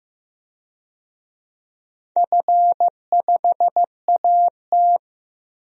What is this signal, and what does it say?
Morse code, heard as a keyed tone, about 15 wpm, 700 Hz. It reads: F5AT
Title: DXCC-100 Call Signs - 4 Characters - Repeated 2x - 15wpm